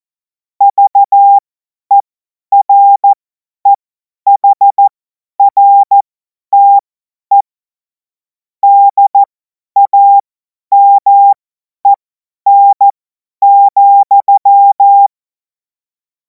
Morse code, heard as a keyed tone, about 14 wpm, spaced as for 10 wpm, 800 Hz. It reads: VEREHRTE DAMEN,
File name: Morsekurs / Witze / Witz 03. März 2026 50 F70